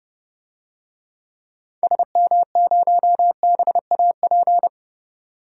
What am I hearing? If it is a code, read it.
Morse code, heard as a keyed tone, about 30 wpm, 700 Hz. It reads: SM0BAP
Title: DXCC-100 Call Signs - 6 Characters - 30wpm